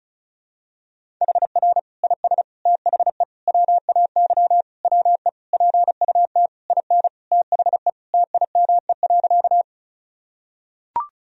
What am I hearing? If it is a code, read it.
Morse code, heard as a keyed tone, about 36 wpm, 700 Hz. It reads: HR IS THE WAY WE PUT IN THE TIME.
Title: Sentences from Top 200 Words and Abbreviations 35wpm